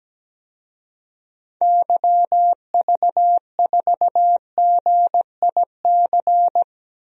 Morse code, heard as a keyed tone, about 17 wpm, 700 Hz. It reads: YV4GIC
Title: DXCC-100 Call Signs - 6 Characters - Repeated 3x - 17wpm